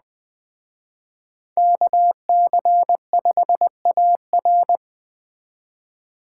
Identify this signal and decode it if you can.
Morse code, heard as a keyed tone, about 21 wpm, 700 Hz. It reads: KC5AR
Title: Speed Racing - USA Call Signs 17wpm